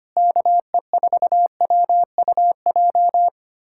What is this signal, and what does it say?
Morse code, heard as a keyed tone, about 25 wpm, 700 Hz. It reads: KE4WUJ
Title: USA Callsign DE Callsign 25wpm